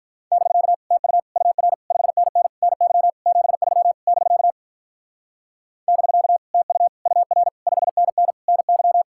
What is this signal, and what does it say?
Morse code, heard as a keyed tone, about 45 wpm, 700 Hz. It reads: <BK> TU UR 5NN NY 73 <BK> <BK> TU UR 5NN NY